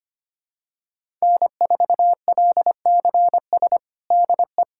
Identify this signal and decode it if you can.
Morse code, heard as a keyed tone, about 25 wpm, 700 Hz. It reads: N4LCS DE
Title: USA Callsign DE Callsign 25wpm